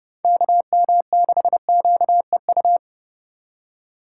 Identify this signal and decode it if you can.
Morse code, heard as a keyed tone, about 30 wpm, 700 Hz. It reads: KM6QEU
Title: USA Call Signs - 6 Characters - 3x Character Spacing - 30wpm